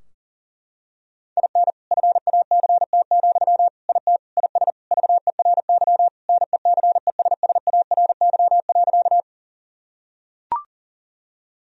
Morse code, heard as a keyed tone, about 37 wpm, 700 Hz. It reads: IN FACT, IT IS VERY NECESSARY.
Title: Sentences from Top 1000 Words - Review Part 3 - 40wpm